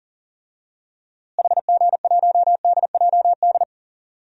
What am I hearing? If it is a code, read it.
Morse code, heard as a keyed tone, about 38 wpm, 700 Hz. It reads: HG1DJD
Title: DXCC-100 Call Signs - 6 Characters - Repeated 3x - 40wpm